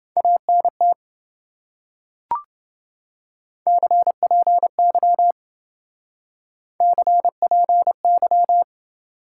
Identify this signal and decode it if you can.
Morse code, heard as a keyed tone, about 28 wpm, 700 Hz. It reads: ANT CPY CPY